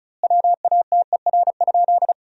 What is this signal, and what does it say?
Morse code, heard as a keyed tone, about 36 wpm, 700 Hz. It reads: WATER?
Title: 2x Word-Spacing - Sentences from Top 200 Words - 35wpm